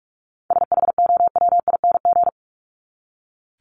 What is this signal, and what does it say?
Morse code, heard as a keyed tone, about 44 wpm, 700 Hz. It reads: SHOWING
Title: ING Suffix - Top 100 Words with Base Words 45wpm